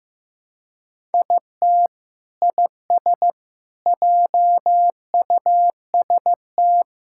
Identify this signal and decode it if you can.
Morse code, heard as a keyed tone, about 16 wpm, 700 Hz. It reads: IT IS JUST
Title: Sentences from Top 1000 Words - Review Part 6 - 15wpm